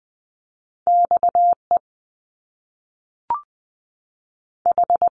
Morse code, heard as a keyed tone, about 20 wpm, 700 Hz. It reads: X E H
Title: Top Two Letter Combinations in Words 20wpm